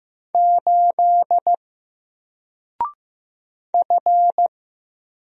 Morse code, heard as a keyed tone, about 15 wpm, 700 Hz. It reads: 8 F
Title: Single Letter-Number Alt 15wpm